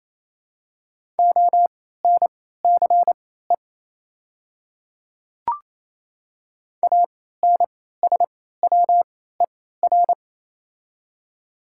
Morse code, heard as a keyed tone, about 27 wpm, spaced as for 16 wpm, 700 Hz. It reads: ONCE ANSWER
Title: 3x Character Spacing - Top 300 Words - 28wpm